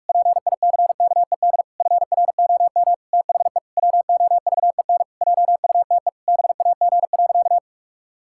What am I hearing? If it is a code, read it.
Morse code, heard as a keyed tone, about 46 wpm, 700 Hz. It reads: PICKED FROM THE WOVEN JUTE BAG.
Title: Pangram Challenge v2 - 45wpm